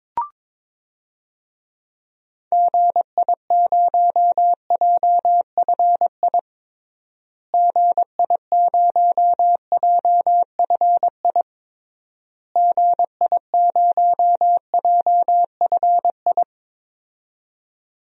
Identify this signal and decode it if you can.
Morse code, heard as a keyed tone, about 22 wpm, 700 Hz. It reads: GI0JFI GI0JFI GI0JFI
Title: DXCC-100 Call Signs - 6 Characters - Repeated 3x - 22wpm